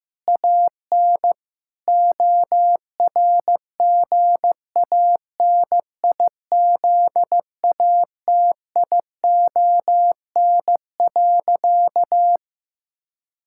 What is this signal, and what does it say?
Morse code, heard as a keyed tone, about 15 wpm, 700 Hz. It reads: AN ORGANIZATION.
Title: Sentences from Top 1000 Words - Review Part 6 - 15wpm